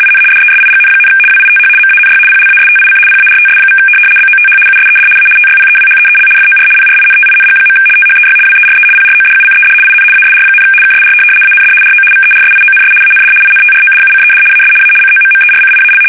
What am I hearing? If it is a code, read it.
RTTY, about 75 baud. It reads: IN DESIGN, WITH MANY PLASTIC PARTS= BOTH USED A TYPEWHEEL FOR PRINTING. THEY WERE PRODUCED IN ASR, KSR AND RO VERSIONS AND INTRODUCED AS A COMMERCIAL
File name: fsk850-75B / teletype-corp